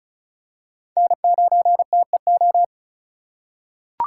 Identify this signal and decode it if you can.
Morse code, heard as a keyed tone, about 36 wpm, 700 Hz. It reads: N9TEO
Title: USA Call Signs - 4x Character Spacing - 35wpm